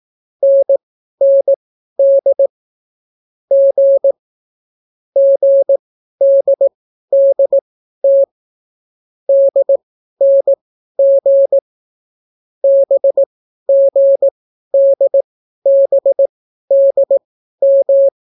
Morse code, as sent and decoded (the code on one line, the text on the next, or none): -. -. -.. / --. / --. -.. -.. - / -.. -. --. / -... --. -.. -... -.. --
NND G GDDT DNG BGDBDM